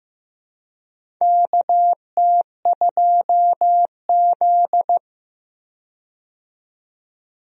Morse code, heard as a keyed tone, about 15 wpm, 700 Hz. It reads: KT2Z
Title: USA Call Signs - 4 Characters - 15wpm